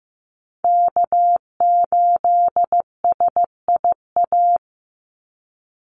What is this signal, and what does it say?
Morse code, heard as a keyed tone, about 15 wpm, 700 Hz. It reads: K8SIA
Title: Call Signs Repeated 3x - 15wpm